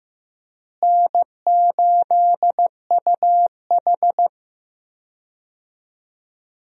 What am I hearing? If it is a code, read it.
Morse code, heard as a keyed tone, about 15 wpm, 700 Hz. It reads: N8UH
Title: USA Call Signs - 4 Characters - Repeated 2x - 15wpm